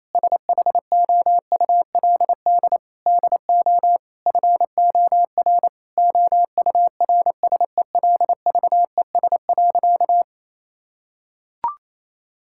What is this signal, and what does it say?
Morse code, heard as a keyed tone, about 28 wpm, 700 Hz. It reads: SHOULD DO FOR OURSELVES.
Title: Sentences from Top 1600 Words 28wpm